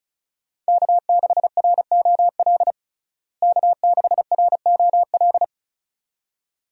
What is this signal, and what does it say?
Morse code, heard as a keyed tone, about 34 wpm, 700 Hz. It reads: K6ROL K6ROL
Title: USA Call Signs - 5 Characters - Repeated 2x - 35wpm